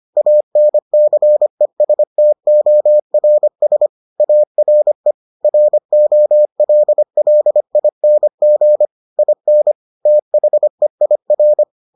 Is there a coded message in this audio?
Yes: Morse code from 0.2 to 11.6 s, heard as a keyed tone, about 25 wpm, 600 Hz. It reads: ANCESTORS ARE ROLLING IN THEIR